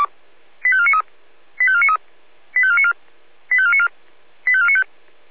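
Five-tone selective call (ZVEI), heard as quick runs of five tones, tones of about 70 ms. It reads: 85493 85493 85494 85494 85495
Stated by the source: ZVEI five-tone selective call received by SDR.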